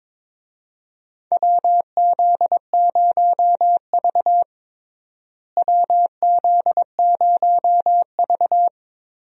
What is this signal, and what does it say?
Morse code, heard as a keyed tone, about 22 wpm, 700 Hz. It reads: WZ0V WZ0V
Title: USA Call Signs - 4 Characters - Repeated 3x - 22wpm